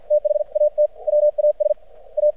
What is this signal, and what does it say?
Morse code, heard as a keyed tone, about 50 wpm, spaced as for 44 wpm, 600 Hz. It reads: THAT WAS A
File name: archived / CRSnet